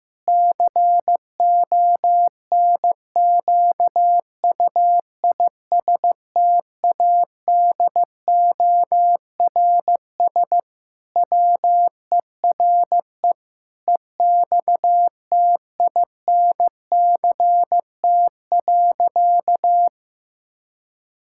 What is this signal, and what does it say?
Morse code, heard as a keyed tone, about 15 wpm, 700 Hz. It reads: CONQUISTADORS WERE EXTINCT.
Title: Pangram Challenge v2 - 15wpm